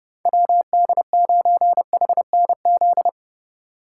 Morse code, heard as a keyed tone, about 30 wpm, 700 Hz. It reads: WD9HNZ